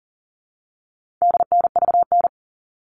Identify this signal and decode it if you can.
Morse code, heard as a keyed tone, about 40 wpm, 700 Hz. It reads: DNVN
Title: All Cut Numbers - 4-Digit 40wpm